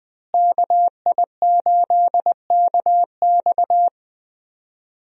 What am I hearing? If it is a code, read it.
Morse code, heard as a keyed tone, about 20 wpm, 700 Hz. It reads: KI8KX
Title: USA Call Signs - 5 Characters - Repeated 3x - 20wpm